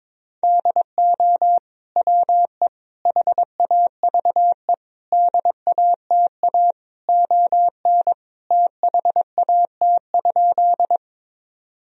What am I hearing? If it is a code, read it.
Morse code, heard as a keyed tone, about 22 wpm, 690 Hz. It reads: DO WE HAVE DATA ON THAT?
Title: Sentences from Top 1000 Words - Review Part 3 - 22wpm